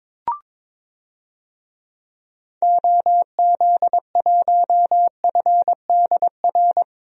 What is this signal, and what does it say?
Morse code, heard as a keyed tone, about 22 wpm, 700 Hz. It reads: OZ1FDR
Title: DXCC-100 Call Signs - 6 Characters - Repeated 2x - 22wpm